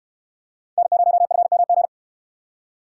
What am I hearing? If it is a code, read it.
Morse code, heard as a keyed tone, about 62 wpm, 700 Hz. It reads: N0FMG